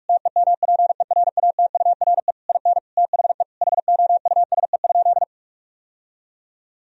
Morse code, heard as a keyed tone, about 46 wpm, 700 Hz. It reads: TEMPERATURE IN THE HOUSE?
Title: Sentences from Top 1000 Words - Review Part 4 - 45wpm